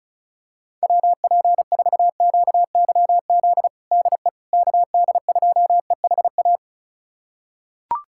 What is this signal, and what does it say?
Morse code, heard as a keyed tone, about 34 wpm, 700 Hz. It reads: WP4QYZ DE KD2EHA
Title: USA Callsign DE Callsign - 6 Characters - 35wpm